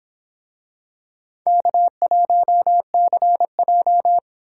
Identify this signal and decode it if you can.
Morse code, heard as a keyed tone, about 26 wpm, 700 Hz. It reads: K1CJ